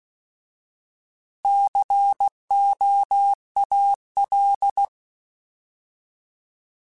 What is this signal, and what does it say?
Morse code, heard as a keyed tone, about 16 wpm, 800 Hz. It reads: COAL
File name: Alphabetical / Alphabetical mp3s / Cs @15